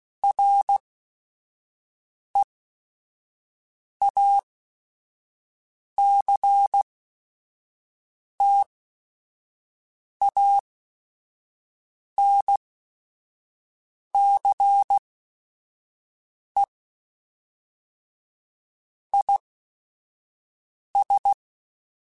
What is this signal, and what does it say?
Morse code, heard as a keyed tone, about 16 wpm, spaced as for 5 wpm, 800 Hz. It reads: REACTANCE IS
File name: Technical Subjects / Technical Subjects-mp3s / XL @5